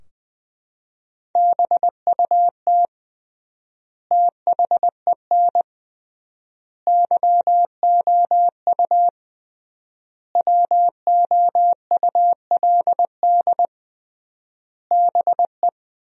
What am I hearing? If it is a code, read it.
Morse code, heard as a keyed tone, about 20 wpm, 700 Hz. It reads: BUT THEN YOU WOULD BE
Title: 3x Word-Spacing - Sentences from Top 300 Words - 20wpm